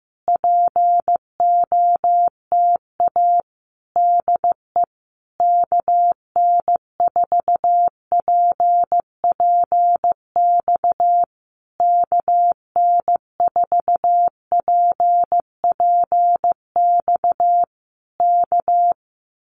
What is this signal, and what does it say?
Morse code, heard as a keyed tone, about 15 wpm, 700 Hz. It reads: POTA DE KN4PPX KN4PPX K
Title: Common QSO Phrases Encore 15wpm